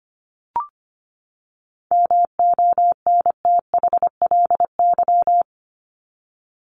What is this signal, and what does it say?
Morse code, heard as a keyed tone, about 25 wpm, 700 Hz. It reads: MONTHLY
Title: LY Suffix - Top 100 Words with Base Words 25wpm